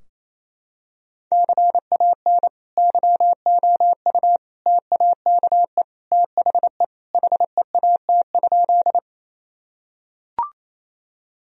Morse code, heard as a keyed tone, about 28 wpm, 700 Hz. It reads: CAN YOU TAKE THE HEAT?
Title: Sentences from Top 1000 Words - Review Part 3 - 28wpm